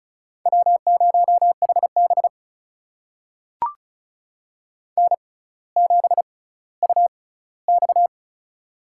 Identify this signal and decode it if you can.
Morse code, heard as a keyed tone, about 36 wpm, spaced as for 18 wpm, 700 Hz. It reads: W0HB N7UX
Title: USA Call Signs - 4 Characters - 6x Character Spacing - 35wpm